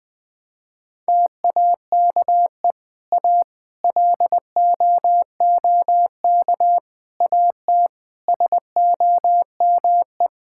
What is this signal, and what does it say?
Morse code, heard as a keyed tone, about 20 wpm, 700 Hz. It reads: TAKE A LOOK AT SOME
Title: Sentences with 7-Word Phrases - Spoken Beginning - 20wpm